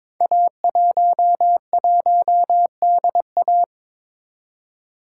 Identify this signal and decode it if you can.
Morse code, heard as a keyed tone, about 22 wpm, 700 Hz. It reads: A11DA
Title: Code Groups - 5 Letter:Numbers - 22wpm